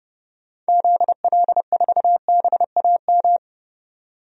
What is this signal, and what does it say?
Morse code, heard as a keyed tone, about 30 wpm, 700 Hz. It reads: ZL4BAM
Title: DXCC-100 Call Signs - 6 Characters - 30wpm